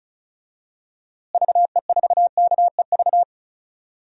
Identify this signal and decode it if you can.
Morse code, heard as a keyed tone, about 36 wpm, 700 Hz. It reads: VE4KEV
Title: DXCC-100 Call Signs - 6 Characters - Repeated 3x - 35wpm